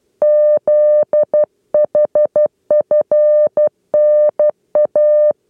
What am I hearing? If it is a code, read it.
Morse code, heard as a keyed tone, about 12 wpm, 600 Hz. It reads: ZHFNA
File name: 8 lezione cw 2